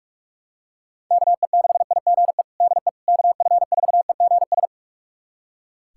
Morse code, heard as a keyed tone, about 42 wpm, 690 Hz. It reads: KE6IGE DE KF4EGS
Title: USA Callsign DE Callsign - 6 Characters - 45wpm